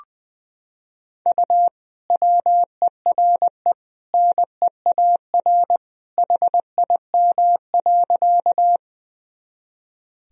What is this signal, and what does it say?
Morse code, heard as a keyed tone, about 20 wpm, 700 Hz. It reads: U WERE NEAR HIM.
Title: Sentences from Top 200 Words and Abbreviations 20wpm